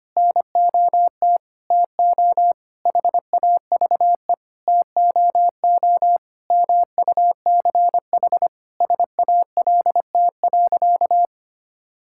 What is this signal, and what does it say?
Morse code, heard as a keyed tone, about 25 wpm, 700 Hz. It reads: NOT TO HAVE TOO MUCH SALT.
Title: Sentences from Top 1000 Words - Review Part 5 - 25wpm